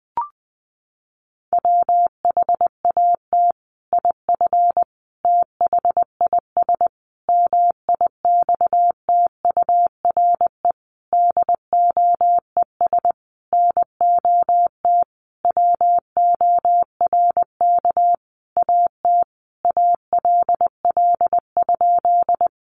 Morse code, heard as a keyed tone, about 20 wpm, 700 Hz. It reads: WHAT IF THIS MIXTURE DOES NOT WORK AT ALL?
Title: Sentences from Top 1500 Words 20wpm